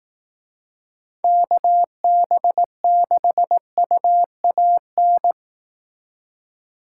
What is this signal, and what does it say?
Morse code, heard as a keyed tone, about 18 wpm, 700 Hz. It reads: KB6UAN
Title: Speed Racing - USA 2x3 Call Signs 15wpm